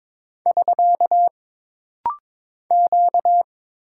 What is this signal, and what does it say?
Morse code, heard as a keyed tone, about 22 wpm, 700 Hz. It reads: <SK> Q
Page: This rapid-fire timing is geared to push your ICR (Instant Character Recognition) skills.